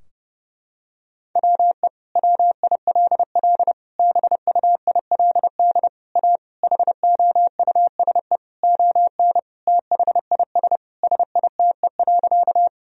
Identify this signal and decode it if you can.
Morse code, heard as a keyed tone, about 30 wpm, 700 Hz. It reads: WE WILL BUILD A HOUSE ON THIS SITE.
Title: Sentences from Top 1000 Words - Review Part 4 - 30wpm